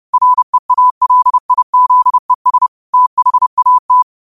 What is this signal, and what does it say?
Morse code, heard as a keyed tone, about 30 wpm, 1000 Hz. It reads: REALIZES THAT
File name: qotdfiles / 30wpm